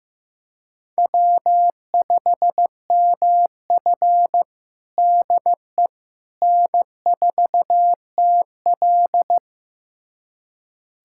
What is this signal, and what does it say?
Morse code, heard as a keyed tone, about 15 wpm, 700 Hz. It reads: W5MF DE N4TL
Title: USA Callsign DE Callsign - 4 Characters - 15wpm